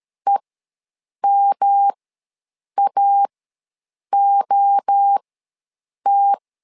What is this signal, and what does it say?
Morse code, heard as a keyed tone, about 13 wpm, spaced as for 7 wpm, 800 Hz. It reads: EMAOT
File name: LEZIONI / 3 MWRO / 3 MWRO ws 12wpm 1914